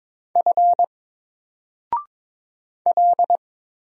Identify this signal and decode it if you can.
Morse code, heard as a keyed tone, about 22 wpm, 700 Hz. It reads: F L
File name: Single-Letter-Number-22wpm